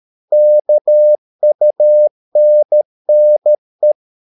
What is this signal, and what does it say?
Morse code, heard as a keyed tone, about 13 wpm, 600 Hz. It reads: KUNNE
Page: Ferie 13 wpm